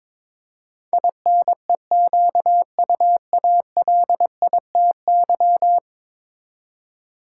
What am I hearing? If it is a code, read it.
Morse code, heard as a keyed tone, about 22 wpm, 700 Hz. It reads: INEQUALITY
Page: morse-code-ninja-practice-22wpm